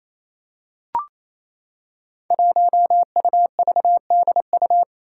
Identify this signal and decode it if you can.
Morse code, heard as a keyed tone, about 28 wpm, 700 Hz. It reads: 1UVDU
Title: Five Challenging Letter-Number Combinations 28wpm